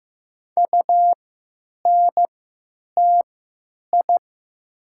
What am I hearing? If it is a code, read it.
Morse code, heard as a keyed tone, about 15 wpm, spaced as for 9 wpm, 700 Hz. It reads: UNTI
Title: 3x Character Spacing - Top 300 Words - 15wpm